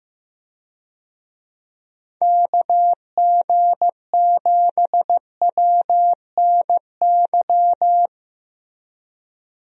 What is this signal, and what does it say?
Morse code, heard as a keyed tone, about 15 wpm, 700 Hz. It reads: KG7WNY
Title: USA Call Signs - 6 Characters - 15wpm